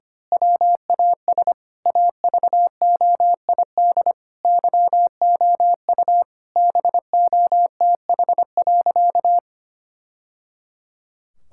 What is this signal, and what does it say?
Morse code, heard as a keyed tone, about 25 wpm, 700 Hz. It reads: WAS AVOID YOU BOTH.
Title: Sentences from Top 1000 Words - Review Part 6 - 25wpm